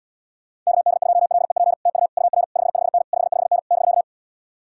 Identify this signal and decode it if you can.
Morse code, heard as a keyed tone, about 66 wpm, 700 Hz. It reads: KD2LEP TU UR 55N 55N <BK>